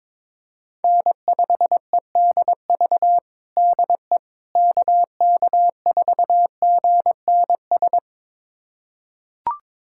Morse code, heard as a keyed tone, about 21 wpm, 700 Hz. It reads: N5EDV DE KK4GNS
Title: USA Callsign DE Callsign 22wpm